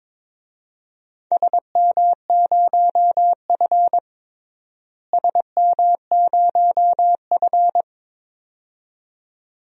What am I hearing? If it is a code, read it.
Morse code, heard as a keyed tone, about 22 wpm, 700 Hz. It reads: SM0F SM0F
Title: DXCC-100 Call Signs - 4 Characters - Repeated 2x - 22wpm